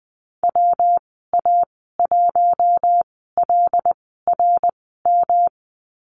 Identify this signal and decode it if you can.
Morse code, heard as a keyed tone, about 20 wpm, spaced as for 14 wpm, 700 Hz. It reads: WA1LRM
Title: 161 Lesson 26g - Letter g - Call Signs 20wpm